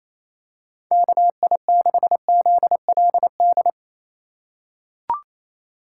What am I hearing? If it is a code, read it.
Morse code, heard as a keyed tone, about 28 wpm, 700 Hz. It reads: KI6ZLD
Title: USA Call Signs - 6 Characters - 3x Character Spacing - 28wpm